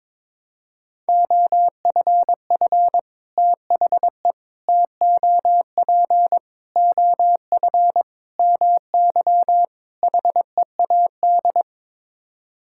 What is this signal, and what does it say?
Morse code, heard as a keyed tone, about 22 wpm, 700 Hz. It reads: OFF THE TOP OF MY HEAD
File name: practice / 22wpm / Sets of 6 Words - 22wpm